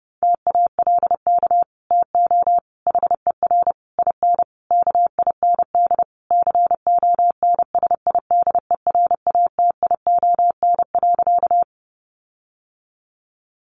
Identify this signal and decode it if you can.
Morse code, heard as a keyed tone, about 30 wpm, 700 Hz. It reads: TALK TO HER IN KIND CONSIDERATION.
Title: Sentences from Top 1500 Words 30wpm